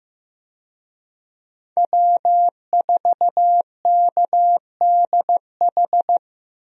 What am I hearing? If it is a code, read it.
Morse code, heard as a keyed tone, about 15 wpm, 700 Hz. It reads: W4KDH
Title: USA Call Signs - 5 Characters - Repeated 2x - 15wpm